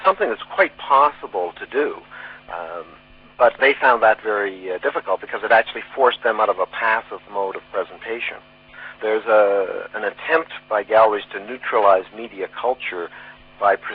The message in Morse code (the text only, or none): none